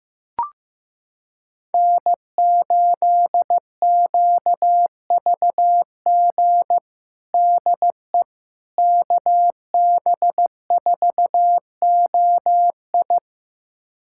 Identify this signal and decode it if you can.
Morse code, heard as a keyed tone, about 15 wpm, 700 Hz. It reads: N8QVG DE KB4OI
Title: USA Callsign DE Callsign - 5 Characters - 15wpm